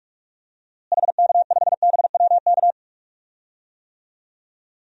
Morse code, heard as a keyed tone, about 47 wpm, 700 Hz. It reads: HK5BWK
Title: DXCC-100 Call Signs - 6 Characters - Repeated 3x - 45wpm